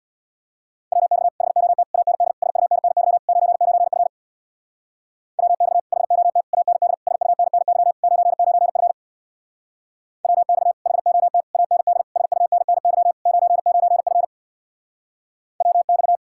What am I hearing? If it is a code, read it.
Morse code, heard as a keyed tone, about 60 wpm, 700 Hz. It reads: WX HOT AND SUNNY 99F WX HOT AND SUNNY 99F WX HOT AND SUNNY 99F WX
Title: Speed-Racing Common QSO Phrases - 50wpm